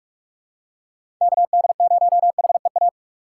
Morse code, heard as a keyed tone, about 44 wpm, 700 Hz. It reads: KD0HEA
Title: USA Call Signs - 6x Character Spacing - 45wpm